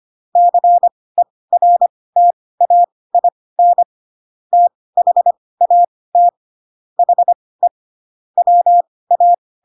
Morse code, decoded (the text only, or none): CERTAIN THAT HE WA